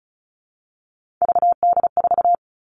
Morse code, heard as a keyed tone, about 34 wpm, 700 Hz. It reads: VD4
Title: All Cut Numbers - 3-Digit 35wpm